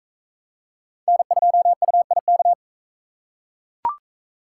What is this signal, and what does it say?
Morse code, heard as a keyed tone, about 37 wpm, 700 Hz. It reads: N2UIK